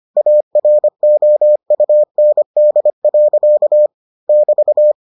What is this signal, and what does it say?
Morse code, heard as a keyed tone, about 25 wpm, 600 Hz. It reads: AROUND. <BT>